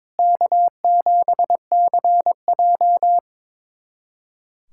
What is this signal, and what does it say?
Morse code, heard as a keyed tone, about 22 wpm, 700 Hz. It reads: K7CJ
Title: USA Callsign DE Callsign - 4 Characters - 22wpm